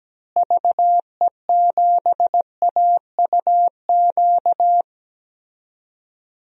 Morse code, heard as a keyed tone, about 17 wpm, 700 Hz. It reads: VE7AUQ
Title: DXCC-100 Call Signs - 6 Characters - Repeated 3x - 17wpm